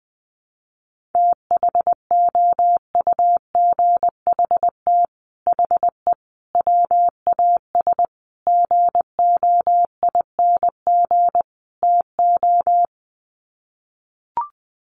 Morse code, heard as a keyed tone, about 20 wpm, 700 Hz. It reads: THOUGHT HE WAS GOING TO
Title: Sets of 5 Words Encore2 20wpm